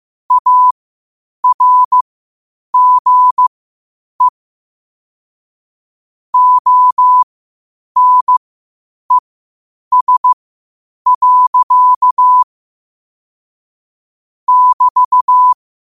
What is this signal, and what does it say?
Morse code, heard as a keyed tone, about 15 wpm, spaced as for 8 wpm, 1000 Hz. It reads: ARGE ONES. <BT>
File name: qotdfiles / 8wpm